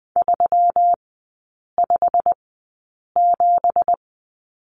Morse code, heard as a keyed tone, about 20 wpm, spaced as for 8 wpm, 700 Hz. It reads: 357
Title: 166 Lesson 27e - Number 7 - Three Characters 20wpm